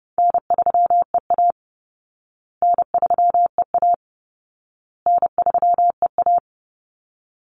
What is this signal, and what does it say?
Morse code, heard as a keyed tone, about 30 wpm, 700 Hz. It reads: N3EA N3EA N3EA
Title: Easy Call Signs Repeated 3x - 30wpm